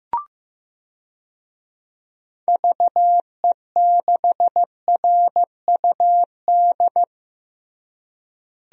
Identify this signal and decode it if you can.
Morse code, heard as a keyed tone, about 15 wpm, 700 Hz. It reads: VE6RUD
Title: DXCC-100 Call Signs - 6 Characters - Repeated 3x - 15wpm